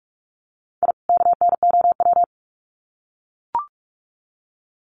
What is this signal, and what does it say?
Morse code, heard as a keyed tone, about 44 wpm, 700 Hz. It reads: I KNOW
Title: Sets of 2 Words 45wpm